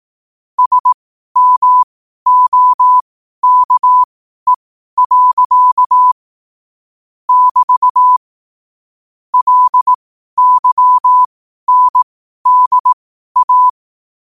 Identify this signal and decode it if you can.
Morse code, heard as a keyed tone, about 18 wpm, spaced as for 12 wpm, 1000 Hz. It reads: SMOKE. <BT> LYNDA
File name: qotdfiles / 12wpm